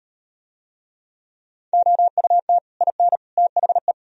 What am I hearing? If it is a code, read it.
Morse code, heard as a keyed tone, about 38 wpm, 700 Hz. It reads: OUT IN THE